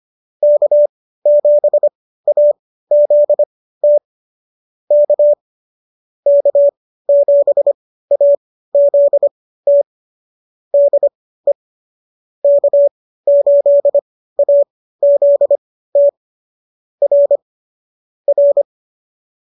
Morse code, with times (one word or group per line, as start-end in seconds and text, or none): 0.4-4.0 K7AZT
4.9-5.3 K
6.3-9.8 K7AZT
10.7-11.5 DE
12.4-16.1 K8AZT
17.0-17.4 R
18.3-18.6 R